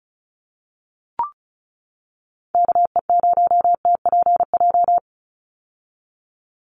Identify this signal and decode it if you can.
Morse code, heard as a keyed tone, about 34 wpm, 700 Hz. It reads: KE0TPJ
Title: USA Call Signs 35wpm